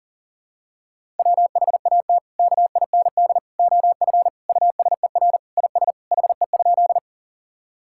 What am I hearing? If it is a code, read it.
Morse code, heard as a keyed tone, about 37 wpm, 700 Hz. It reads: WHAT KIND OF USER IS HE?
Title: Sentences from Top 1000 Words - Review Part 5 - 40wpm